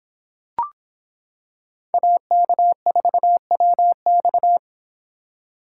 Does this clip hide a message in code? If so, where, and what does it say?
Yes: Morse code from 1.9 to 4.6 s, heard as a keyed tone, about 26 wpm, 700 Hz. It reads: AK4WX